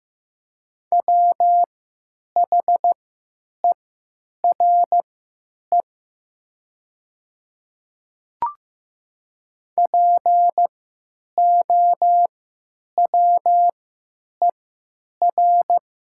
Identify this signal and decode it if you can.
Morse code, heard as a keyed tone, about 15 wpm, spaced as for 9 wpm, 700 Hz. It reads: WHERE POWER
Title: 3x Character Spacing - Top 500 Words - 15wpm